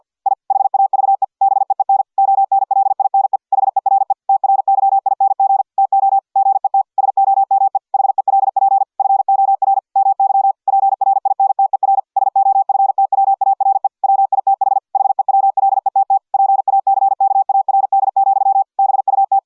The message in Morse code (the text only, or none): I HAVE BEEN ONLINE HERE TRYING TO GET SOME HELP FOR MY PRINTER SOFTWARE WITH HEWLETT PACKARD, BUT